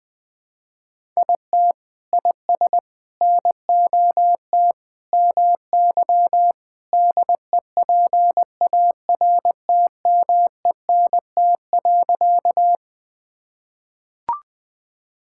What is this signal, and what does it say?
Morse code, heard as a keyed tone, about 21 wpm, 700 Hz. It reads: IT IS NOT MY DEPARTMENT.
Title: Sentences from Top 1000 Words - Review Part 5 - 20wpm